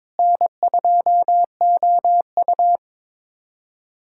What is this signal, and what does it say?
Morse code, heard as a keyed tone, about 22 wpm, 700 Hz. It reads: N2OU
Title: USA Callsign DE Callsign - 4 Characters - 22wpm